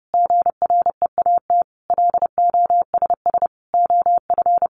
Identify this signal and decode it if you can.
Morse code, heard as a keyed tone, about 30 wpm, 700 Hz. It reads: GREAT LOSS OF
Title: Sentences from Top 1400 Words 30wpm